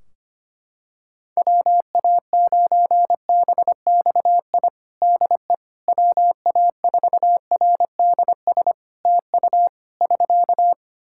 Morse code, heard as a keyed tone, about 25 wpm, 700 Hz. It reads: WA9BXI DE WA4RDS TU <SK>
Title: Practice QSO - Exchange 4 - Ending - 25wpm